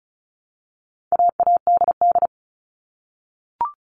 Morse code, heard as a keyed tone, about 34 wpm, 700 Hz. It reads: AADD
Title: All Cut Numbers - 4-Digit 35wpm